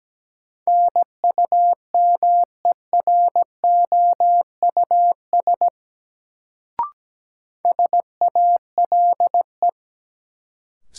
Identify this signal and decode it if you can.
Morse code, heard as a keyed tone, about 17 wpm, 700 Hz. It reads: NUMEROUS SALE
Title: Sentences from Top 1400 Words - New Word Drill - 17wpm